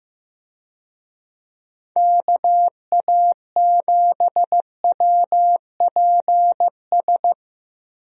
Morse code, heard as a keyed tone, about 15 wpm, 700 Hz. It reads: KA7WPS
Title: USA Call Signs - 6 Characters - Repeated 2x - 15wpm